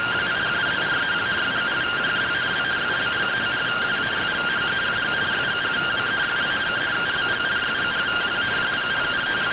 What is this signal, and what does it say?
Two-tone teleprinter signal, about 170 Hz shift.